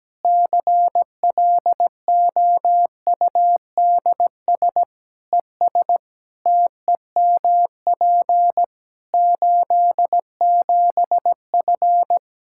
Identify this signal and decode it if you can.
Morse code, heard as a keyed tone, about 17 wpm, 700 Hz. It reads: CLOUDS ES TEMP 87F
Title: Practice QSO - Exchange 2 - Info - 17wpm